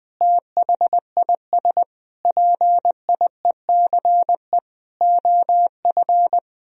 Morse code, heard as a keyed tone, about 21 wpm, 700 Hz. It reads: THIS PIECE OF
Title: Sentences from Top 1000 Words - Review Part 2 - 20wpm